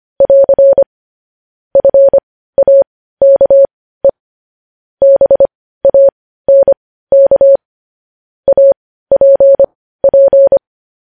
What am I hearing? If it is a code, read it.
Morse code, heard as a keyed tone, about 25 wpm, spaced as for 15 wpm, 550 Hz. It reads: <AR> FAKE BANK APP
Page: news headlines in morse code at 15 wpm